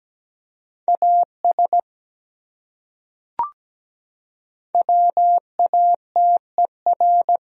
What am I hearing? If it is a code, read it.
Morse code, heard as a keyed tone, about 17 wpm, 700 Hz. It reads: AS WATER
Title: Top 100 Words 17wpm